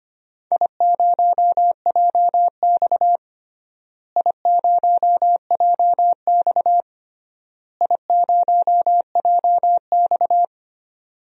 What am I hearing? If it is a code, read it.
Morse code, heard as a keyed tone, about 25 wpm, 700 Hz. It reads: I0JX I0JX I0JX
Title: DXCC-100 Call Signs - 4 Characters - Repeated 3x - 25wpm